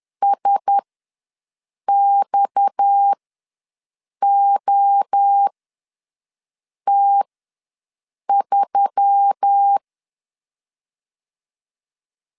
Morse code, heard as a keyed tone, about 10 wpm, spaced as for 5 wpm, 800 Hz. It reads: SXOT3
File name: LEZIONI / 7 GF9Z / 7 GF ws 10wpm 3918